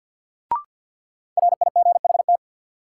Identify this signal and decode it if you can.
Morse code, heard as a keyed tone, about 53 wpm, 700 Hz. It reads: RIGHT
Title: Sentences from Top 200 Words - New Word Drill - 50wpm